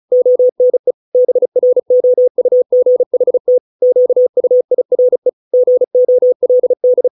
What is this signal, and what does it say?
Morse code, heard as a keyed tone, about 35 wpm, 500 Hz. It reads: ONE BROUGHT QUIRE GOLD
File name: 632 words / 632 Words 35WPM